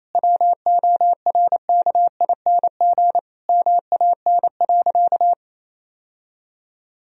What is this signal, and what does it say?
Morse code, heard as a keyed tone, about 28 wpm, 700 Hz. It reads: WORKING MAN.
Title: Sentences from Top 1000 Words - Review Part 4 - 28wpm